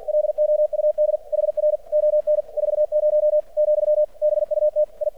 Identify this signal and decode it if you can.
Morse code, heard as a keyed tone, about 50 wpm, 600 Hz. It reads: LOUD HR ON 40 , BUT I
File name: archived / CRSnet